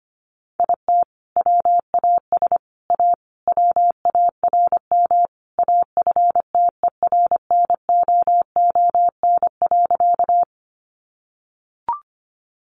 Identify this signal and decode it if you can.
Morse code, heard as a keyed tone, about 25 wpm, 700 Hz. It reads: IT WAS A WARM AFTERNOON.
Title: Sentences from Top 1500 Words 25wpm